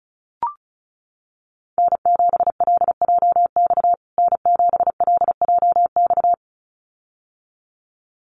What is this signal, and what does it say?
Morse code, heard as a keyed tone, about 34 wpm, 700 Hz. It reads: N7LJX N7LJX
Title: USA Call Signs Repeated 2x 35wpm